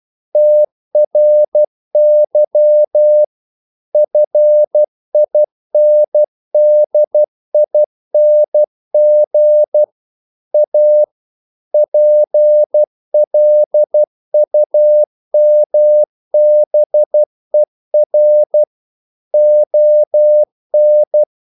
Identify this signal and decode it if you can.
Morse code, heard as a keyed tone, about 12 wpm, 600 Hz. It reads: TRY FINDING A PLUMBER ON
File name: CW / Practice / EnglishQuotations0312